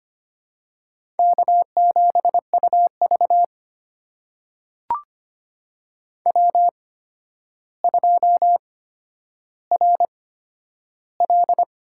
Morse code, heard as a keyed tone, about 25 wpm, spaced as for 11 wpm, 700 Hz. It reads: K7UV W2RL
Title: USA Call Signs - 4 Characters - 8x Character Spacing - 25wpm